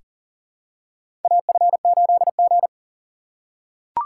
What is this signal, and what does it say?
Morse code, heard as a keyed tone, about 43 wpm, 690 Hz. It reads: AF8G
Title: USA Call Signs - 4 Characters - 6x Character Spacing - 40wpm